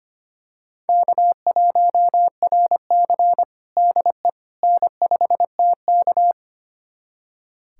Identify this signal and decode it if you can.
Morse code, heard as a keyed tone, about 25 wpm, 700 Hz. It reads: K1RC DE N5TK
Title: USA Callsign DE Callsign - 4 Characters - 25wpm